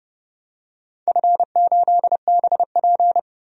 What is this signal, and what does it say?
Morse code, heard as a keyed tone, about 30 wpm, 700 Hz. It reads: F8BP
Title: DXCC-100 Call Signs - 4 Characters - 30wpm